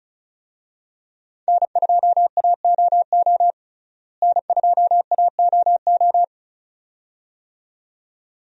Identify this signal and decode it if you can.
Morse code, heard as a keyed tone, about 34 wpm, 700 Hz. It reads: N2AOO N2AOO
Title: USA Call Signs - 5 Characters - Repeated 2x - 35wpm